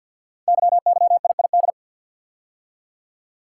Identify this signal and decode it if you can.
Morse code, heard as a keyed tone, about 50 wpm, 700 Hz. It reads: YYIID
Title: Code Groups - 5 Letter:Numbers - 50wpm